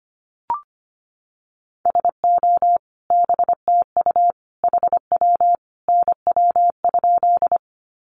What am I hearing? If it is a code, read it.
Morse code, heard as a keyed tone, about 25 wpm, 700 Hz. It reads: SO BTU HW NW?
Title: Common QSO Phrases 25wpm